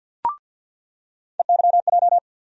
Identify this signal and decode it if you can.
Morse code, heard as a keyed tone, about 47 wpm, 700 Hz. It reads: EYJ
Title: USA Call Sign Postfixes - 3 Letters 50wpm